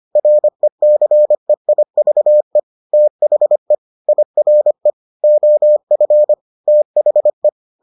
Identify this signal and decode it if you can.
Morse code, heard as a keyed tone, about 25 wpm, 600 Hz. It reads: RECEIVE THE IRE OF THE